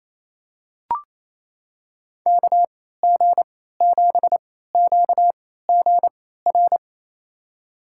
Morse code, heard as a keyed tone, about 28 wpm, spaced as for 16 wpm, 700 Hz. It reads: KG7QGR
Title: USA Call Signs - 6 Characters - 3x Character Spacing - 28wpm